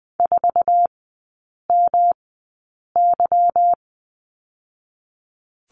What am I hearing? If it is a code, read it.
Morse code, heard as a keyed tone, about 20 wpm, spaced as for 8 wpm, 700 Hz. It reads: 4MY
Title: 218 Lesson 35e - Number 0 - Three Characters 20wpm